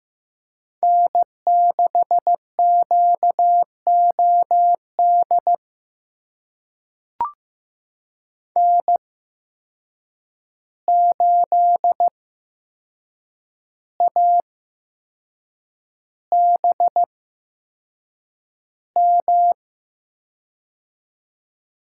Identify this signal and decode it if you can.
Morse code, heard as a keyed tone, about 15 wpm, spaced as for 6 wpm, 700 Hz. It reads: N6QOD N8ABM
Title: USA Call Signs - 5 Characters - 8x Character Spacing - 15wpm